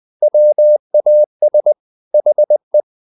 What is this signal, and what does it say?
Morse code, heard as a keyed tone, about 20 wpm, 600 Hz. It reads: WAS HE
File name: the old man and the sea part 2 20wpm